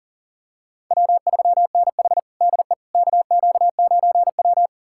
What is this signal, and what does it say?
Morse code, heard as a keyed tone, about 40 wpm, 700 Hz. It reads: W3NH DE KQ9W
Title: USA Callsign DE Callsign - 4 Characters - 40wpm